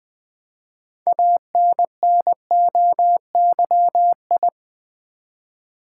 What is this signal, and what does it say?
Morse code, heard as a keyed tone, about 20 wpm, 700 Hz. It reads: ANNOYI
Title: Build A Word - 3 Letters At A Time - 8-Letter Words - 20wpm